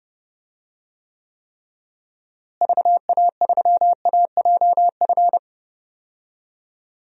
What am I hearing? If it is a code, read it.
Morse code, heard as a keyed tone, about 30 wpm, 700 Hz. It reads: VA3AJF
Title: DXCC-100 Call Signs - 6 Characters - Repeated 3x - 30wpm